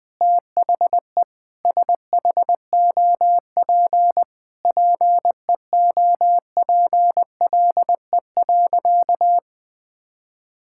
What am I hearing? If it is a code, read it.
Morse code, heard as a keyed tone, about 21 wpm, 700 Hz. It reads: THE SHOP PEOPLE.
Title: Sentences from Top 1600 Words 20wpm